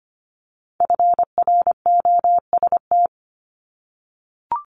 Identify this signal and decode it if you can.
Morse code, heard as a keyed tone, about 25 wpm, 700 Hz. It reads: FROST
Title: Multi-letter Phonemes 25wpm